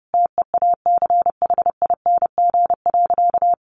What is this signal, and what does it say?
Morse code, heard as a keyed tone, about 30 wpm, 700 Hz. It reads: TEACHING.
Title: Sentences from Top 1100 Words 30wpm